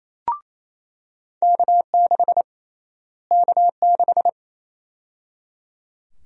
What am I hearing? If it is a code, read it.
Morse code, heard as a keyed tone, about 27 wpm, 700 Hz. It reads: K6 K6
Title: USA Call Sign Prefixes - Repeated 2x - 28wpm